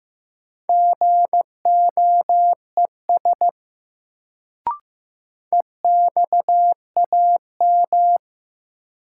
Morse code, heard as a keyed tone, about 15 wpm, 700 Hz. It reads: GOES EXAM
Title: Sentences from Top 1000 Words - Review Part 5 - New Word Drill - 15wpm